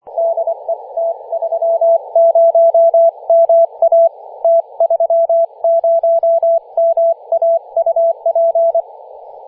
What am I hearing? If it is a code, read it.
Morse code, heard as a keyed tone, about 25 wpm, 670 Hz. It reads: DE T30MA T30MA UP
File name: posts / 2016-nov-10-0528z-t30ma-21-007mhz-cw